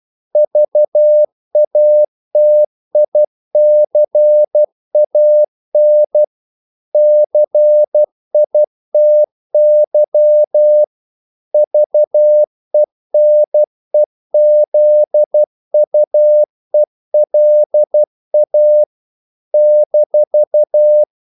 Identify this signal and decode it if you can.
Morse code, heard as a keyed tone, about 12 wpm, 600 Hz. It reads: VATICAN CITY VENEZUELA -
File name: CW / Practice / NationalCapitals12